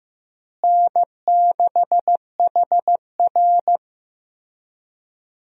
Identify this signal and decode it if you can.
Morse code, heard as a keyed tone, about 15 wpm, 700 Hz. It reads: N6HR
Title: USA Call Signs - 4 Characters - Repeated 2x - 15wpm